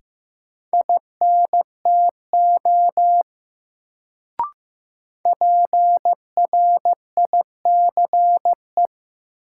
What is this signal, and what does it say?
Morse code, heard as a keyed tone, about 15 wpm, 700 Hz. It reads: INTO PRICE
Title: Sentences from Top 600 Words - New Word Drill - 15wpm